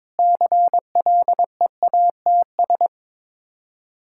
Morse code, heard as a keyed tone, about 22 wpm, 700 Hz. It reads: CLEATS
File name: Difficult Words with Hints - 22wpm